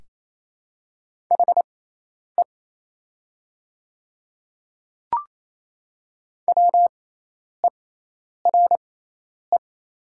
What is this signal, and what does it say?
Morse code, heard as a keyed tone, about 28 wpm, spaced as for 10 wpm, 700 Hz. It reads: HE WERE
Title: 6x Character Spacing - Top 200 Words - 28wpm